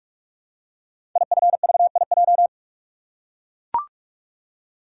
Morse code, heard as a keyed tone, about 44 wpm, 700 Hz. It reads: IFVIJ
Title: Code Groups - 5 Letters - 45wpm